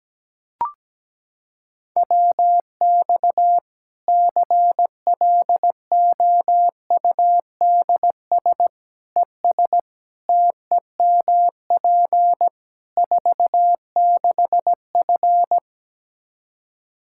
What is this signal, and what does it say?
Morse code, heard as a keyed tone, about 17 wpm, 700 Hz. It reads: WX CLOUDS ES TEMP 46F
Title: Practice QSO - Exchange 2 - Info - 17wpm